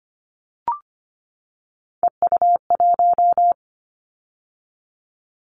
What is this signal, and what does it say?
Morse code, heard as a keyed tone, about 25 wpm, 700 Hz. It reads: EU1
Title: Callsign Prefixes for Top 50 DXCC - 25wpm